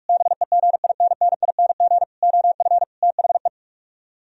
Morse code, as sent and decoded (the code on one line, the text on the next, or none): -... --. .. -. -. .. -. --. / --- ..-. / - .... .
BEGINNING OF THE